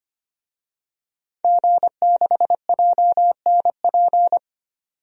Morse code, heard as a keyed tone, about 25 wpm, 700 Hz. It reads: G6JNP
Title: DXCC-100 Call Signs - 5 Characters - 25wpm